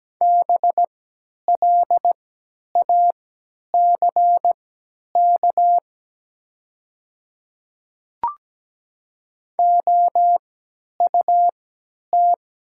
Morse code, heard as a keyed tone, about 17 wpm, spaced as for 10 wpm, 700 Hz. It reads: BLACK OUT
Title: 3x Character Spacing - Top 500 Words - 17wpm